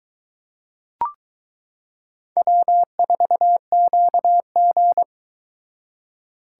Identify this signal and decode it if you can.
Morse code, heard as a keyed tone, about 21 wpm, 700 Hz. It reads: W4QG